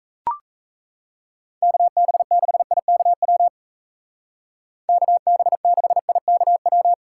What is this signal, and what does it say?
Morse code, heard as a keyed tone, about 34 wpm, 700 Hz. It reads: KB6IKW KB6IKW
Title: Speed Racing - USA 2x3 Call Signs 28wpm